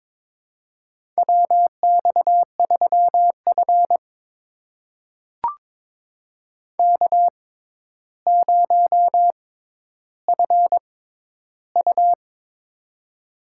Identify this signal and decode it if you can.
Morse code, heard as a keyed tone, about 22 wpm, spaced as for 11 wpm, 700 Hz. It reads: WX3F K0FU
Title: USA Call Signs - 4 Characters - 6x Character Spacing - 22wpm